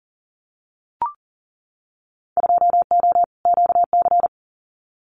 Morse code, heard as a keyed tone, about 40 wpm, 700 Hz. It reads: 2O QC
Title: Winter Field Day Exchanges 40wpm